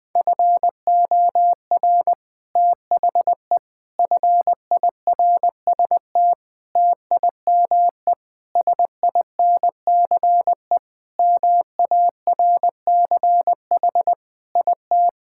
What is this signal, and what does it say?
Morse code, heard as a keyed tone, about 19 wpm, 700 Hz. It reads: FOR THE FIRST TIME SINCE MARCH IT
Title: Sentences with 5-Word Phrases - 20wpm